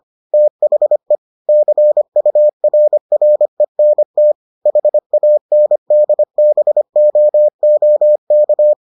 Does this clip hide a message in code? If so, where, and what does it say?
Yes: Morse code from 0.3 to 8.7 s, heard as a keyed tone, about 25 wpm, 600 Hz. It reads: THE CURRENT HANDBOOK